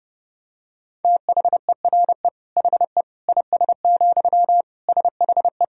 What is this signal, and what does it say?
Morse code, heard as a keyed tone, about 30 wpm, 700 Hz. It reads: THERE HE IS, SHE
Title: Sentences from Top 100 Words and Abbreviations 30wpm